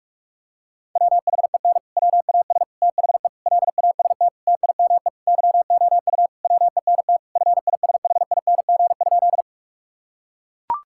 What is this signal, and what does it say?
Morse code, heard as a keyed tone, about 42 wpm, 700 Hz. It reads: WHEN WAS THE LAST TIME YOU WENT FISHING?
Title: Sentences from Top 1000 Words - Review Part 5 - 45wpm